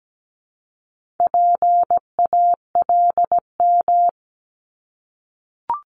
Morse code, heard as a keyed tone, about 17 wpm, 700 Hz. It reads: PALM
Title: Multi-letter Phonemes 17wpm